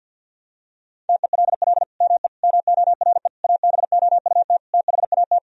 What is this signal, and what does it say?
Morse code, heard as a keyed tone, about 50 wpm, 700 Hz. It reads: TELL ME MORE ABOUT THAT
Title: Sentences from Top 1000 Words - Review Part 4 - 50wpm